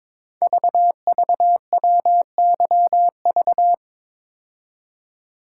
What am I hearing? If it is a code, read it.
Morse code, heard as a keyed tone, about 21 wpm, 700 Hz. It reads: VVWYV
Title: Code Groups - 5 Letters - 22wpm